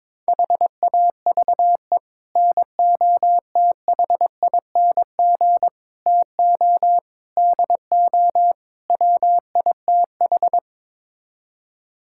Morse code, heard as a keyed tone, about 22 wpm, 700 Hz. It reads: HAVE NOTHING TO DO WITH
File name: practice / 22wpm / Build A Phrase - 5-Word Phrases - 22wpm